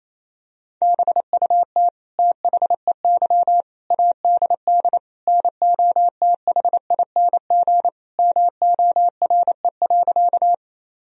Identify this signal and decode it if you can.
Morse code, heard as a keyed tone, about 28 wpm, 700 Hz. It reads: BUT THEY ADD NOTHING MORE.
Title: Sentences from Top 1000 Words - Review Part 1 - 28wpm